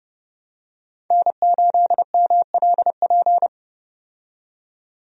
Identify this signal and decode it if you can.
Morse code, heard as a keyed tone, about 30 wpm, 700 Hz. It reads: N8MLP
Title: USA Call Signs - 5 Characters - 30wpm